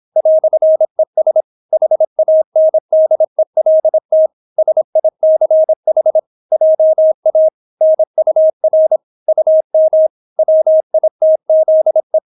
Morse code, heard as a keyed tone, about 26 wpm, 630 Hz. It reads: "ES HANDELT SICH JA NUR UM WITZE